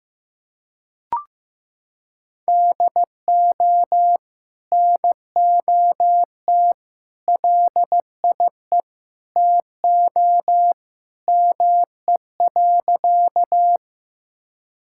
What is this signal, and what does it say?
Morse code, heard as a keyed tone, about 15 wpm, 700 Hz. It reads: DO NOT LIE TO ME.
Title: Sentences from Top 1600 Words 15wpm